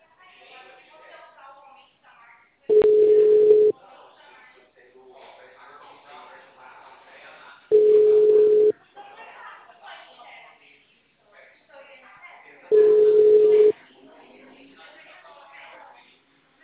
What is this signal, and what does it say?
Ringback tone (Europe, 425 Hz), with speech.
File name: out- -2001- - - .882